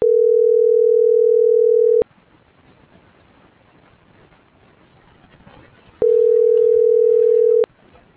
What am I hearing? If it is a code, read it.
Ringback tone (North America), 440 + 480 Hz.
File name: CENTRAL GENERAL / out yarit.rdz - .72766